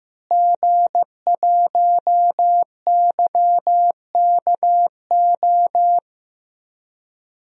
Morse code, heard as a keyed tone, about 15 wpm, 700 Hz. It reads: G1YKO
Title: DXCC-100 Call Signs - 5 Characters - Repeated 3x - 15wpm